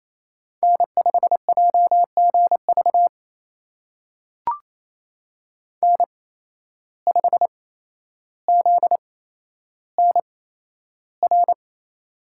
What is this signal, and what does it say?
Morse code, heard as a keyed tone, about 28 wpm, spaced as for 12 wpm, 700 Hz. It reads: N5JGV N5ZNR
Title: USA Call Signs - 5 Characters - 8x Character Spacing - 28wpm